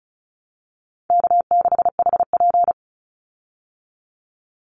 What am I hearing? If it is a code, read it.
Morse code, heard as a keyed tone, about 34 wpm, 700 Hz. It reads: K6HP
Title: Easy Call Signs Repeated 3x - 35wpm